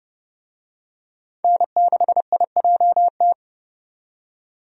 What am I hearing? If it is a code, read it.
Morse code, heard as a keyed tone, about 30 wpm, 700 Hz. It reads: N6IJT
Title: USA Call Signs - 5 Characters - 30wpm